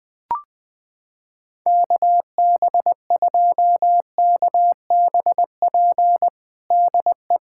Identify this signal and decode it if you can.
Morse code, heard as a keyed tone, about 20 wpm, 700 Hz. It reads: KB2KBP DE
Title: USA Callsign DE Callsign - 6 Characters - 20wpm